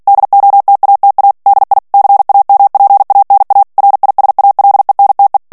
Morse code, heard as a keyed tone, about 48 wpm, 800 Hz. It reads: DOTATA DI CAMPANA RISALENTE
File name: morse mp3 / ITALIEN texte clair 1 vitesse 45 mots minute